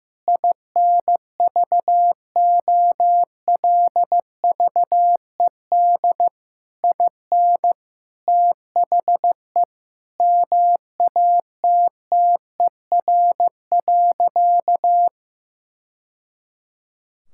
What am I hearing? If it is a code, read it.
Morse code, heard as a keyed tone, about 15 wpm, 700 Hz. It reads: INVOLVED IN THE MATTER.
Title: Sentences from Top 1000 Words - Review Part 5 - 15wpm